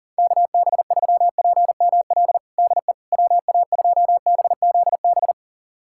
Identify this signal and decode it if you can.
Morse code, heard as a keyed tone, about 40 wpm, 700 Hz. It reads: KB3PML DE WA2BZB
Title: USA Callsign DE Callsign - 6 Characters - 40wpm